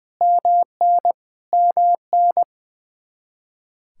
Morse code, heard as a keyed tone, about 20 wpm, 700 Hz. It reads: MN MN
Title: Practice QSO - Exchange 1 - Beginning - 20wpm